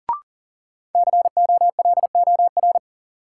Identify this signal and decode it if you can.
Morse code, heard as a keyed tone, about 37 wpm, 700 Hz. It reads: COLOR
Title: Sentences from Top 1600 Words - New Word Drill Alt - 40wpm